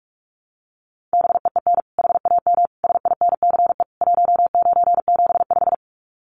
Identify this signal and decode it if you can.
Morse code, heard as a keyed tone, about 44 wpm, 700 Hz. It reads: BEEN HAM SINCE 1975
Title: Common QSO Phrases 45wpm